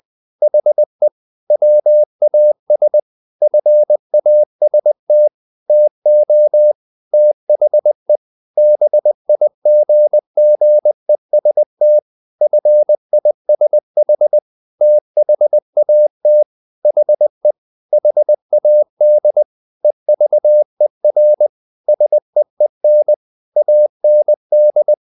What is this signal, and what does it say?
Morse code, heard as a keyed tone, about 20 wpm, 600 Hz. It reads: HE WAS FAST TO THE BIGGEST FISH THAT HE HAD EVER SEEN AND
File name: the old man and the sea / the old man and the sea part 1 20wpm